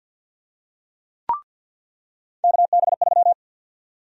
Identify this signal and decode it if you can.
Morse code, heard as a keyed tone, about 46 wpm, 700 Hz. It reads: KB3
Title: USA Call Sign Prefixes - 50wpm